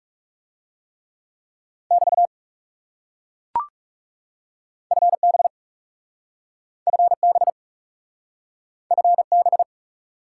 Morse code, heard as a keyed tone, about 37 wpm, 700 Hz. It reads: <BT> FB FB FB
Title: Speed-Racing Top 90 QSO Elements 30wpm